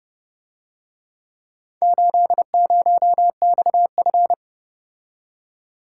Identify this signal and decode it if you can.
Morse code, heard as a keyed tone, about 31 wpm, 700 Hz. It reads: 80XF